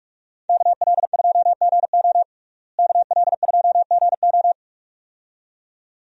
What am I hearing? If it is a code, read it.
Morse code, heard as a keyed tone, about 46 wpm, 700 Hz. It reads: KL2GO KL2GO
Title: USA Call Signs - 5 Characters - Repeated 2x - 45wpm